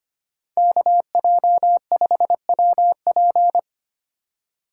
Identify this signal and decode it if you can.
Morse code, heard as a keyed tone, about 25 wpm, 700 Hz. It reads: KJ5WP
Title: USA Call Signs - 5 Characters - 2x Character Spacing - 25wpm